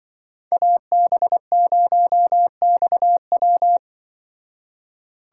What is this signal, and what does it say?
Morse code, heard as a keyed tone, about 25 wpm, 700 Hz. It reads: AB0XW
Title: Speed Racing - USA 2x2 Call Signs 20wpm